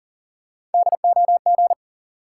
Morse code, heard as a keyed tone, about 40 wpm, 700 Hz. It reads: DOG
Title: Sentences from Top 400 Words - New Word Drill - 40wpm